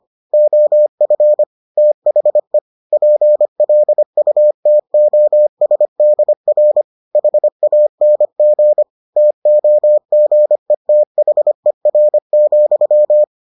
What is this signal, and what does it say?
Morse code, heard as a keyed tone, about 25 wpm, 600 Hz. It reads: OF THE PLUTOSDR HANG TOGETHER,